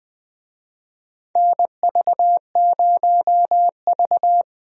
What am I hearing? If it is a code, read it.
Morse code, heard as a keyed tone, about 20 wpm, 700 Hz. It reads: NV0V
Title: USA Call Signs - 4 Characters - Repeated 2x - 20wpm